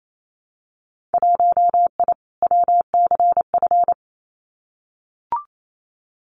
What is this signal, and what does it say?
Morse code, heard as a keyed tone, about 28 wpm, 700 Hz. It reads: 1I WCF
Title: Winter Field Day Exchanges Encore 28wpm